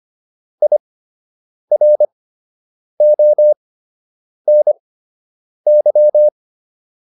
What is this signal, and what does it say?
Morse code, heard as a keyed tone, about 25 wpm, spaced as for 8 wpm, 600 Hz. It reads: IRONY